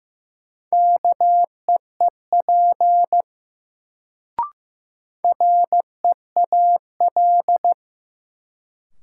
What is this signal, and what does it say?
Morse code, heard as a keyed tone, about 15 wpm, 700 Hz. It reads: KEEP REAL
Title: Sentences from Top 300 Words - New Word Drill - 15wpm